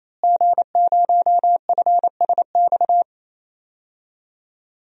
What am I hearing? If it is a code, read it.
Morse code, heard as a keyed tone, about 28 wpm, 700 Hz. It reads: G0FSX
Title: DXCC-100 Call Signs - 5 Characters - Repeated 3x - 28wpm